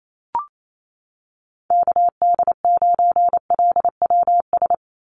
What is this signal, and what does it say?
Morse code, heard as a keyed tone, about 28 wpm, 700 Hz. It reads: KD9LWS
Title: USA Call Signs 28wpm